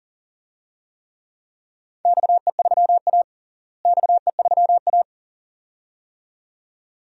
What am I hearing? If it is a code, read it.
Morse code, heard as a keyed tone, about 40 wpm, 700 Hz. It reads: XE3A XE3A
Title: DXCC-100 Call Signs - 4 Characters - Repeated 2x - 40wpm